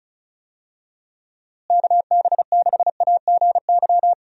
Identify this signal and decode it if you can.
Morse code, heard as a keyed tone, about 36 wpm, 700 Hz. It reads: KB6AGY
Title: USA Call Signs - 6 Characters - Repeated 3x - 35wpm